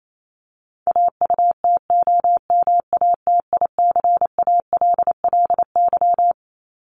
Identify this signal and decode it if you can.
Morse code, heard as a keyed tone, about 28 wpm, 700 Hz. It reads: AUTOMATICALLY
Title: Top 1751-2000 Words 28wpm